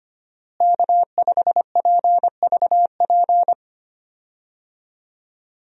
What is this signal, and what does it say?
Morse code, heard as a keyed tone, about 25 wpm, 700 Hz. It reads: K5PVP
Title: USA Call Signs - 5 Characters - Repeated 3x - 25wpm